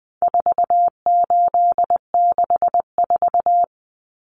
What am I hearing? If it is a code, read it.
Morse code, heard as a keyed tone, about 20 wpm, 700 Hz. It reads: 4864
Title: Numbers - 4-Digit 20wpm